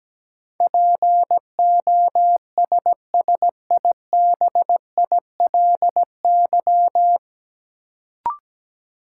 Morse code, heard as a keyed tone, about 17 wpm, 700 Hz. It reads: POSSIBILY
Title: Sentences from Top 1300 Words - New Word Drill - 17wpm